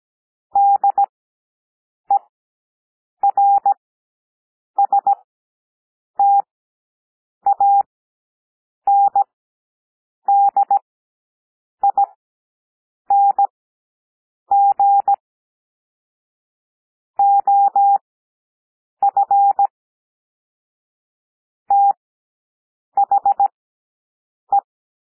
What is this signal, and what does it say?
Morse code, heard as a keyed tone, about 17 wpm, spaced as for 7 wpm, 800 Hz. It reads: DERSTANDING OF THE
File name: SOUND PLAIN 17 wpm